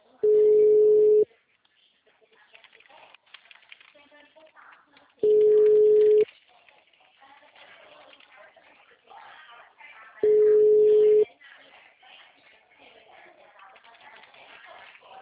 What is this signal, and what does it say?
Ringback tone (Europe, 425 Hz), with speech.
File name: out- -2013- - - .52487